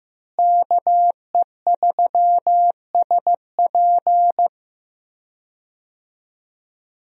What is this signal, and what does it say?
Morse code, heard as a keyed tone, about 15 wpm, 700 Hz. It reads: KE3SP
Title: USA Call Signs - 5 Characters - Repeated 3x - 15wpm